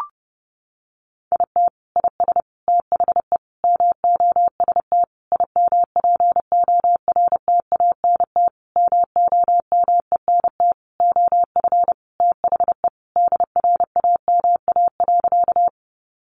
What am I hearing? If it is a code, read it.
Morse code, heard as a keyed tone, about 30 wpm, 700 Hz. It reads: IT IS THE MOST IMPORTANT MOMENT OF THE DRAMA.
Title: Sentences from Top 1500 Words 30wpm